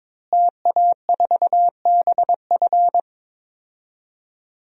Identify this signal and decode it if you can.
Morse code, heard as a keyed tone, about 21 wpm, 700 Hz. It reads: TA4BF
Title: DXCC-100 Call Signs - 5 Characters - Repeated 2x - 22wpm